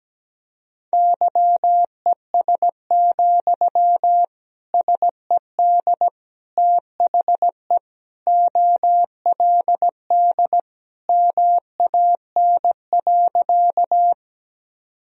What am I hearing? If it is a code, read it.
Morse code, heard as a keyed tone, about 17 wpm, 700 Hz. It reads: YES, SED THE OLD MAN.
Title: Sentences from Top 500 Words and Abbreviations 17wpm